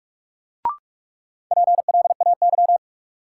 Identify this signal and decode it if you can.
Morse code, heard as a keyed tone, about 45 wpm, 700 Hz. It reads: PLAY
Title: Sentences from Top 200 Words - New Word Drill - 45wpm